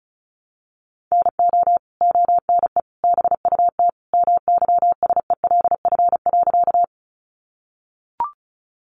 Morse code, heard as a keyed tone, about 34 wpm, 700 Hz. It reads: NO ONE BUT MYSELF.
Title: Sentences from Top 1300 Words 35wpm